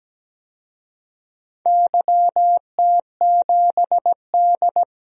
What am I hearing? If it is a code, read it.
Morse code, heard as a keyed tone, about 17 wpm, 700 Hz. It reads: YT7D
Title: DXCC-100 Call Signs - 4 Characters - 17wpm